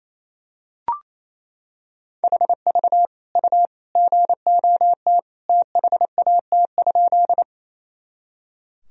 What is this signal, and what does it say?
Morse code, heard as a keyed tone, about 28 wpm, 700 Hz. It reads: HV U GOT THAT?
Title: Sentences from Top 300 Words and Abbreviations 28wpm